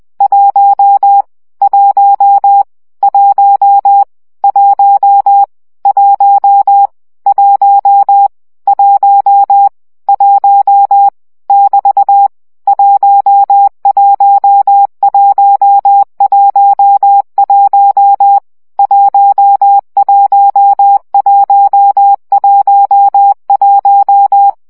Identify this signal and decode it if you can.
Morse code, heard as a keyed tone, about 20 wpm, 800 Hz. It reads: 1 1 1 1 1 1 1 1 <BT> 11111 11111
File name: CW gyakorlas 1 17 baud